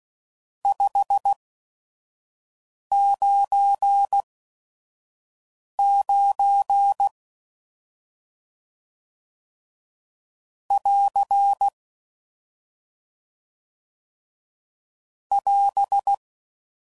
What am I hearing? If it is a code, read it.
Morse code, heard as a keyed tone, about 16 wpm, spaced as for 5 wpm, 800 Hz. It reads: 599 <AR> <AS>
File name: CW Code Groups @5WPM / RST's / Various RST No's